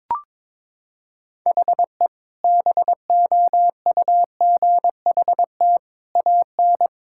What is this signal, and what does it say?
Morse code, heard as a keyed tone, about 21 wpm, 700 Hz. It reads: HE BOUGHT AN
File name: practice / 22wpm / Sentences with 2-Word Phrases - 22wpm